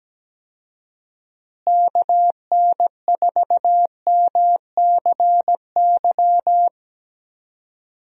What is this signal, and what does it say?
Morse code, heard as a keyed tone, about 17 wpm, 700 Hz. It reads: KN4MCY
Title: USA Call Signs - 6 Characters - Repeated 2x - 17wpm